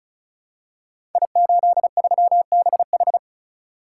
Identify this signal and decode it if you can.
Morse code, heard as a keyed tone, about 36 wpm, 690 Hz. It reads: I83BH
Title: Code Groups - 5 Letter:Numbers - 35wpm